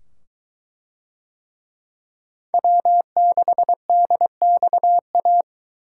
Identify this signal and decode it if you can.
Morse code, heard as a keyed tone, about 21 wpm, 700 Hz. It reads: W6DXA